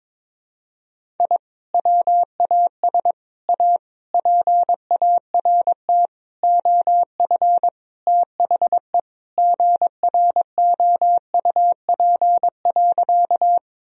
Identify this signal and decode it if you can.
Morse code, heard as a keyed tone, about 22 wpm, 700 Hz. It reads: I WAS A PART OF THE GROUP.
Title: Sentences from Top 1000 Words - Review Part 1 - 22wpm